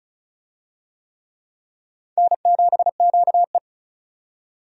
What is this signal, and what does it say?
Morse code, heard as a keyed tone, about 36 wpm, 700 Hz. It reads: N7QE
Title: USA Call Signs - 4 Characters - 35wpm